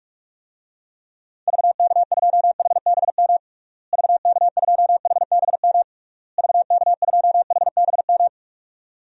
Morse code, heard as a keyed tone, about 41 wpm, 700 Hz. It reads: VK2HBM VK2HBM VK2HBM
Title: DXCC-100 Call Signs - 6 Characters - Repeated 3x - 45wpm